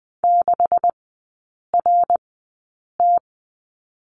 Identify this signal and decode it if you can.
Morse code, heard as a keyed tone, about 20 wpm, spaced as for 8 wpm, 700 Hz. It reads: 6RT
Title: 127 Lesson 21e - Number 6 - Three Characters 20wpm